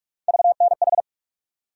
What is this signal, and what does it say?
Morse code, heard as a keyed tone, about 42 wpm, 700 Hz. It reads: VNH
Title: USA Call Sign Postfixes - 3 Letters 45wpm